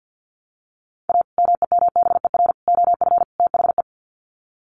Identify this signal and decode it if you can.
Morse code, heard as a keyed tone, about 50 wpm, 700 Hz. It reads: A MEMBER OF THE
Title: Sets of 4 Words 50wpm